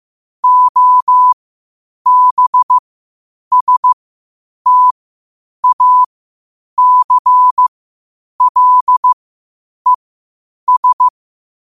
Morse code, heard as a keyed tone, about 15 wpm, spaced as for 8 wpm, 1000 Hz. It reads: OBSTACLES
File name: qotdfiles / 8wpm